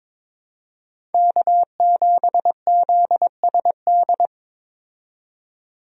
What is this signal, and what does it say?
Morse code, heard as a keyed tone, about 21 wpm, 700 Hz. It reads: K7ZSD
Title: USA Call Signs - 5 Characters - 22wpm